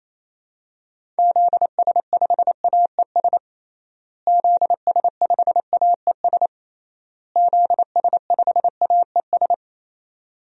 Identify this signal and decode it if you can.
Morse code, heard as a keyed tone, about 28 wpm, 700 Hz. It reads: ZS5AES ZS5AES ZS5AES
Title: DXCC-100 Call Signs - 6 Characters - Repeated 3x - 28wpm